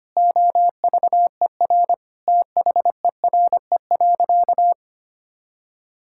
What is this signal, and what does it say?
Morse code, heard as a keyed tone, about 25 wpm, 700 Hz. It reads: OVER THERE.
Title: Sentences from Top 1600 Words 25wpm